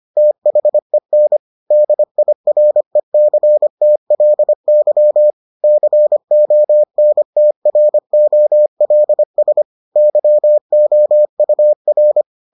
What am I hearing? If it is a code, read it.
Morse code, heard as a keyed tone, about 25 wpm, 600 Hz. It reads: THEN DIRECTLY CONTROLS YOUR